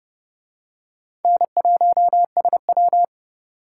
Morse code, heard as a keyed tone, about 27 wpm, 700 Hz. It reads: N1SW